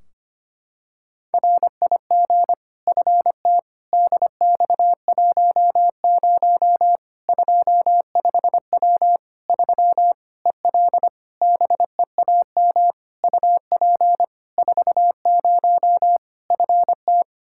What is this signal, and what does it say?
Morse code, heard as a keyed tone, about 25 wpm, 700 Hz. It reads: RIG FT DX10 25W 3 EL BEAM UP 40 FT
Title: Practice QSO - Exchange 2 - Info - 25wpm